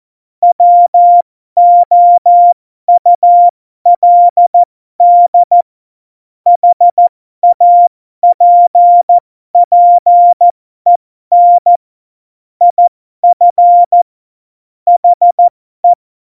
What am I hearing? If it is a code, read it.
Morse code, heard as a keyed tone, about 14 wpm, spaced as for 12 wpm, 700 Hz. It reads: WOULD HAPPEN IF HE
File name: Morsekurs / Witze